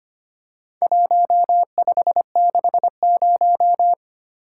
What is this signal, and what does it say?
Morse code, heard as a keyed tone, about 25 wpm, 700 Hz. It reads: 1560
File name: practice / 25wpm / Four Letter-Number Combinations - 25wpm